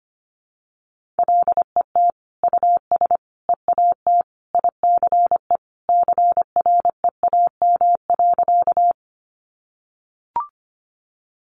Morse code, heard as a keyed tone, about 25 wpm, 700 Hz. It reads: LET US EAT ICE CREAM.
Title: Sentences from Top 1500 Words 25wpm